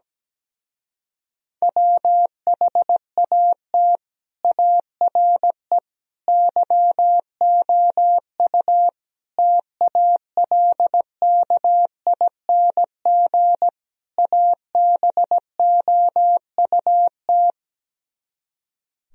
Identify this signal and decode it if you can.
Morse code, heard as a keyed tone, about 17 wpm, 700 Hz. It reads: WHAT ARE YOU TALKING ABOUT
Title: Speed-Racing - Sets of 5 Words 17wpm